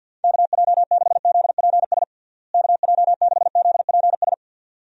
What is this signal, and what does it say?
Morse code, heard as a keyed tone, about 47 wpm, 700 Hz. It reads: KJ6ZPS KJ6ZPS
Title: USA Call Signs - 6 Characters - Repeated 2x - 50wpm